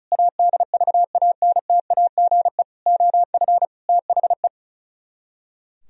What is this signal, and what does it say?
Morse code, heard as a keyed tone, about 43 wpm, 700 Hz. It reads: ADVANTAGE OF THE
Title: Speed-Racing - Sets of 5 Words 35wpm